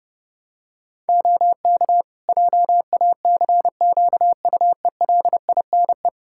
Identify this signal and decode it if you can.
Morse code, heard as a keyed tone, about 30 wpm, 700 Hz. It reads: OK JACQUELINE
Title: K1USN SST QSO Practice 30wpm